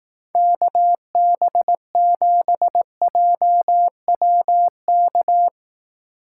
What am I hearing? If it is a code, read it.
Morse code, heard as a keyed tone, about 18 wpm, 700 Hz. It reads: KB7JWK
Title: Speed Racing - USA 2x3 Call Signs 15wpm